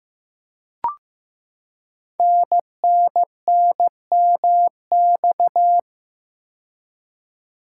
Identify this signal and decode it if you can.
Morse code, heard as a keyed tone, about 15 wpm, 700 Hz. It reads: NNNMX
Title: Code Groups - 5 Letters - 15wpm